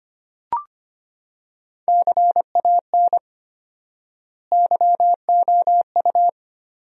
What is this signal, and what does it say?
Morse code, heard as a keyed tone, about 25 wpm, 700 Hz. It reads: CAN YOU
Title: 4x Word-Spacing - Sentences from Top 200 Words - 25wpm